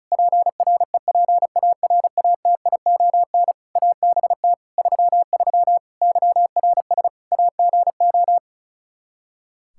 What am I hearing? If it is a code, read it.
Morse code, heard as a keyed tone, about 36 wpm, 700 Hz. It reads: PREPARATION ABT 33 YRS AGO
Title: K1USN SST QSO Practice 35wpm